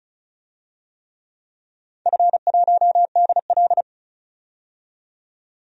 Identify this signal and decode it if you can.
Morse code, heard as a keyed tone, about 36 wpm, 700 Hz. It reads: F1DL
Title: DXCC-100 Call Signs - 4 Characters - 35wpm